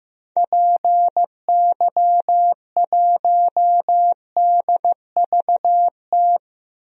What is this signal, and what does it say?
Morse code, heard as a keyed tone, about 15 wpm, 700 Hz. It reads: PY1DVT
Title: DXCC-100 Call Signs - 6 Characters - Repeated 3x - 15wpm